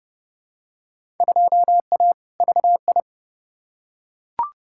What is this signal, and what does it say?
Morse code, heard as a keyed tone, about 30 wpm, 700 Hz. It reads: 2A VI
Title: Field Day Exchanges - 30wpm